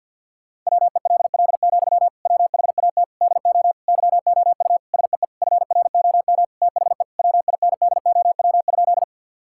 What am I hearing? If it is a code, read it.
Morse code, heard as a keyed tone, about 50 wpm, 700 Hz. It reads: WELL, WHAT DO YOU SEE FROM THE WINDOW?
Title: Sentences from Top 1600 Words 50wpm